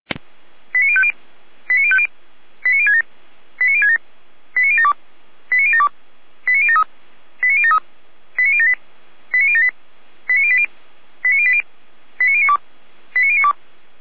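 Five-tone selective call (ZVEI), heard as quick runs of five tones, tones of about 70 ms. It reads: 89050 89050 89067 89067 89072 89072 89073 89073 89078 89078 89080 89080 89092 89092
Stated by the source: Received on SDR as ZVEI five-tone selective call, address 89012.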